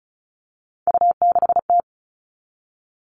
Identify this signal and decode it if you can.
Morse code, heard as a keyed tone, about 34 wpm, 700 Hz. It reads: U6T
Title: All Cut Numbers - 3-Digit 35wpm